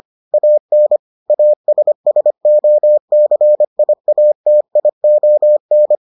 Morse code, heard as a keyed tone, about 25 wpm, 600 Hz. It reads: AN ASSOCIATION